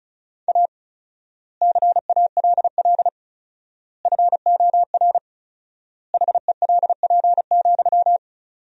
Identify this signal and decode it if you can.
Morse code, heard as a keyed tone, about 34 wpm, 700 Hz. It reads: A CALL FOR HELP,
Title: 4x Word-Spacing - Sentences from Top 300 Words - 35wpm